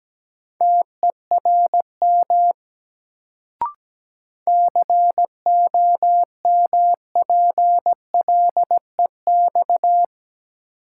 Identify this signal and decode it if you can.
Morse code, heard as a keyed tone, about 17 wpm, 700 Hz. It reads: TERM COMPLEX
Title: Sentences from Top 900 Words - New Word Drill - 17wpm